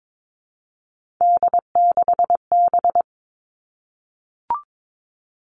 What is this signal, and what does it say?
Morse code, heard as a keyed tone, about 22 wpm, 700 Hz. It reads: D6B
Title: Three Challenging Letter-Number Combinations 22wpm